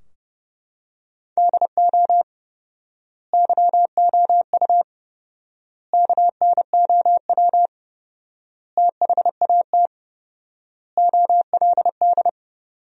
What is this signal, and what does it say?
Morse code, heard as a keyed tone, about 30 wpm, 700 Hz. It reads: DO YOU KNOW THAT OLD
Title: 4x Word-Spacing - Sentences from Top 400 Words - 30wpm